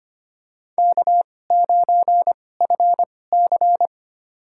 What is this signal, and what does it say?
Morse code, heard as a keyed tone, about 25 wpm, spaced as for 18 wpm, 700 Hz. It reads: K9FC
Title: USA Call Signs - 4 Characters - 2x Character Spacing - 25wpm